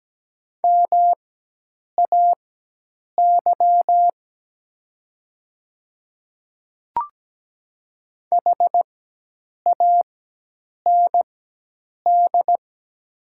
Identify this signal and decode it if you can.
Morse code, heard as a keyed tone, about 17 wpm, spaced as for 8 wpm, 700 Hz. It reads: MAY HAND
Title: 4x Character Spacing - Top 200 Words - 17wpm